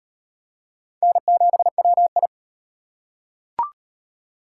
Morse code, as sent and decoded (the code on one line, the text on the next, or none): -. --... .-- ..
N7WI